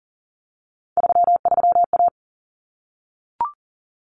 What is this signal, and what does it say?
Morse code, heard as a keyed tone, about 40 wpm, 700 Hz. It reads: 33A
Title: Common Cut Numbers - 3-Digit 40wpm